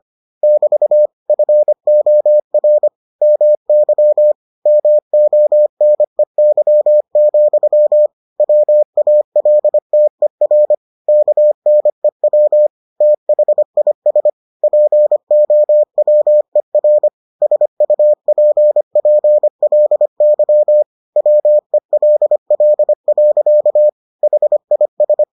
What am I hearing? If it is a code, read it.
Morse code, heard as a keyed tone, about 25 wpm, 600 Hz. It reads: <BT> FOR MY MONEY, WALTER KNEW THIS POWER SUPPLY WELL. HIS